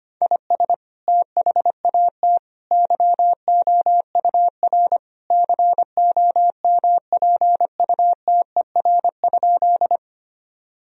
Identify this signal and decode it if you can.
Morse code, heard as a keyed tone, about 25 wpm, 700 Hz. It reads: IS THAT YOUR COMPUTER?
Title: Sentences from Top 1000 Words - Review Part 3 - 25wpm